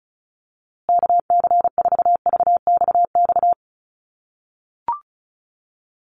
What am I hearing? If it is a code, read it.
Morse code, heard as a keyed tone, about 34 wpm, 700 Hz. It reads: KC4VXX
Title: USA Call Signs Repeated 2x 35wpm